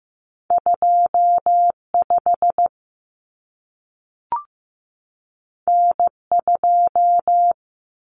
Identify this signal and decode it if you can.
Morse code, heard as a keyed tone, about 15 wpm, 700 Hz. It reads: 25 N2
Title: Common Cut Numbers - 2-Digit 15wpm